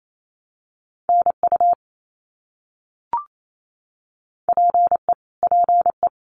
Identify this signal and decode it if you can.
Morse code, heard as a keyed tone, about 28 wpm, 700 Hz. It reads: NU PE PE
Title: Canadian Provinces and Territories - Repeated 3x - 28wpm